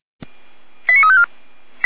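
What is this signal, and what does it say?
Five-tone selective call (ZVEI), heard as quick runs of five tones, tones of about 70 ms. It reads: 89354
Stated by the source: ZVEI five-tone call 89353 received on an SDR.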